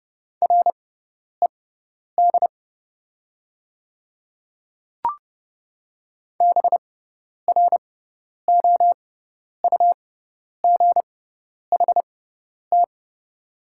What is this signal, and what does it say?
Morse code, heard as a keyed tone, about 30 wpm, spaced as for 10 wpm, 700 Hz. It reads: RED BROUGHT
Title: 6x Character Spacing - Top 500 Words - 30wpm